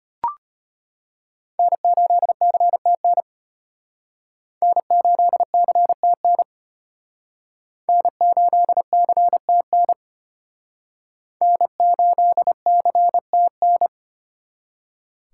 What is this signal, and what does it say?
Morse code, heard as a keyed tone, about 31 wpm, 700 Hz. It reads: N8CTN N8CTN N8CTN N8CTN